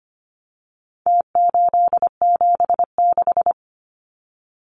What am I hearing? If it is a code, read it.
Morse code, heard as a keyed tone, about 25 wpm, 700 Hz. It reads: T876
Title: Numbers - Contest Style - 4-Digit 25wpm